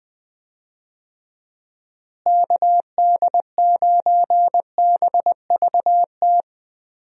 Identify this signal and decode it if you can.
Morse code, heard as a keyed tone, about 19 wpm, 700 Hz. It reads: KD9BVT
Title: USA Call Signs - 6 Characters - Repeated 2x - 20wpm